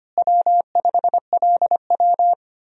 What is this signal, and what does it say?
Morse code, heard as a keyed tone, about 25 wpm, 700 Hz. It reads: W5LW
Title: USA Call Signs - 4 Characters - Repeated 3x - 25wpm